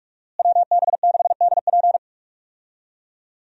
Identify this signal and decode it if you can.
Morse code, heard as a keyed tone, about 46 wpm, 700 Hz. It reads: WB6DP
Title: USA Call Signs - 5 Characters - Repeated 2x - 45wpm